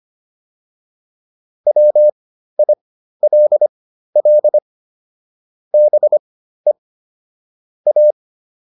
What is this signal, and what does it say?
Morse code, heard as a keyed tone, about 25 wpm, spaced as for 13 wpm, 600 Hz. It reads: WILL BE A
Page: intermediate-practice-files